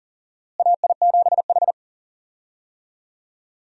Morse code, heard as a keyed tone, about 40 wpm, 700 Hz. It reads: AI7H
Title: USA Call Signs - 4 Characters - Repeated 2x - 40wpm